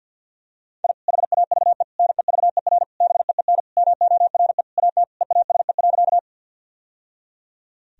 Morse code, heard as a keyed tone, about 52 wpm, 700 Hz. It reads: I HAVE NEVER BEEN MORE AT EASE.
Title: Sentences from Top 1000 Words - Review Part 1 - 50wpm